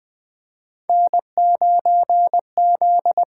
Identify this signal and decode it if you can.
Morse code, heard as a keyed tone, about 20 wpm, 700 Hz. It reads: N9Z
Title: USA Call Signs - 5 Characters - 8x Character Spacing - 20wpm